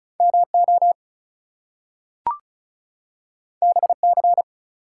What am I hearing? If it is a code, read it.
Morse code, heard as a keyed tone, about 36 wpm, 700 Hz. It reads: MO BC
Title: US States and Canadian Provinces 35wpm